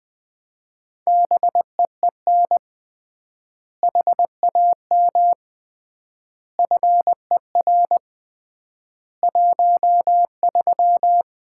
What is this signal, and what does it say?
Morse code, heard as a keyed tone, about 20 wpm, 700 Hz. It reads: BEEN HAM FER 13
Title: Common QSO Phrases - 3x Word-Spacing - 20wpm